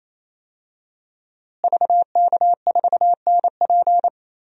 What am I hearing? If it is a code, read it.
Morse code, heard as a keyed tone, about 28 wpm, 700 Hz. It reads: VK4NP
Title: DXCC-100 Call Signs - 5 Characters - Repeated 3x - 28wpm